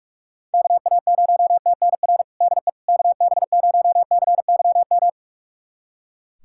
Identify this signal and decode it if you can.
Morse code, heard as a keyed tone, about 41 wpm, 700 Hz. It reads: KA0TNR DE KB0CYM
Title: USA Callsign DE Callsign 45wpm